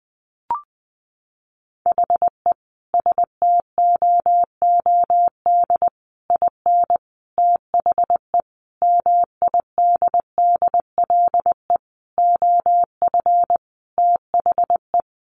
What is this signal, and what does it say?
Morse code, heard as a keyed tone, about 20 wpm, 700 Hz. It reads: HE STOOD IN THE MIDDLE OF THE
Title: Sentences from Top 1400 Words 20wpm